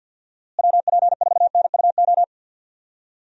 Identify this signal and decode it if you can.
Morse code, heard as a keyed tone, about 45 wpm, 700 Hz. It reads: WP4NUO
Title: USA Call Signs - 6 Characters - Repeated 3x - 50wpm